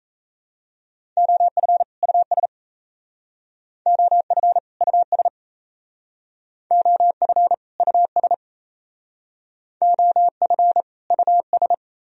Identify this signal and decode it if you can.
Morse code, heard as a keyed tone, about 37 wpm, 700 Hz. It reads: OF US OF US OF US OF US